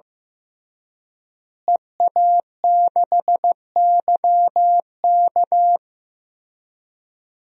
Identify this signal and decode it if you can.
Morse code, heard as a keyed tone, about 15 wpm, 700 Hz. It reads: EA6YK
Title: DXCC-100 Call Signs - 5 Characters - Repeated 3x - 15wpm